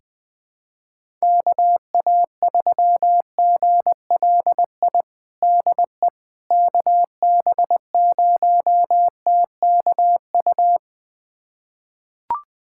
Morse code, heard as a keyed tone, about 21 wpm, 700 Hz. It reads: KA3GLI DE KB0TKU
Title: USA Callsign DE Callsign - 6 Characters - 20wpm